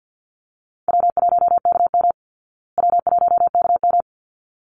Morse code, heard as a keyed tone, about 50 wpm, 700 Hz. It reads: W1KM W1KM
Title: Call Signs Repeated 3x - 50wpm